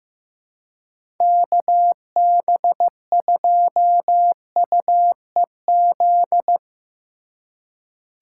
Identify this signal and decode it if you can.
Morse code, heard as a keyed tone, about 15 wpm, 700 Hz. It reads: KB2UEZ
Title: USA Call Signs - 6 Characters - 3x Character Spacing - 15wpm